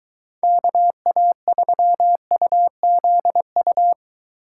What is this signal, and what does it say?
Morse code, heard as a keyed tone, about 23 wpm, 700 Hz. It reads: KA3UZU